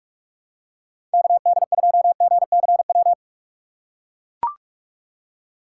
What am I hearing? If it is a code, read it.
Morse code, heard as a keyed tone, about 44 wpm, 700 Hz. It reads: KD2GCW
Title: USA Call Signs - 6 Characters - 6x Character Spacing - 45wpm